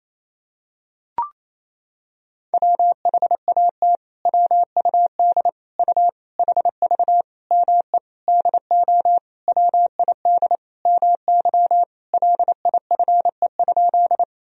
Morse code, heard as a keyed tone, about 28 wpm, 700 Hz. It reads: WHAT WUD U HV ME DO WID MY LIFE?
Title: Sentences from Top 300 Words and Abbreviations 28wpm